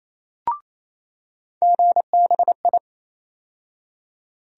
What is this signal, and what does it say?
Morse code, heard as a keyed tone, about 27 wpm, 700 Hz. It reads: GBI
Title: USA Call Sign Postfixes - 3 Letters 28wpm